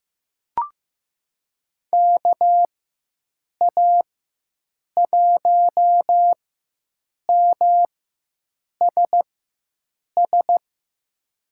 Morse code, heard as a keyed tone, about 16 wpm, spaced as for 7 wpm, 700 Hz. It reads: KA1MSS
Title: USA Call Signs - 4x Character Spacing - 15wpm